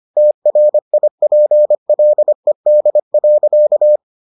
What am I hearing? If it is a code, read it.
Morse code, heard as a keyed tone, about 25 wpm, 600 Hz. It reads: TRIPLED.